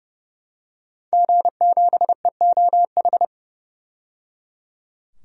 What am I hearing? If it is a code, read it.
Morse code, heard as a keyed tone, about 30 wpm, 700 Hz. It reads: G7EOH
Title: DXCC-100 Call Signs - 5 Characters - 30wpm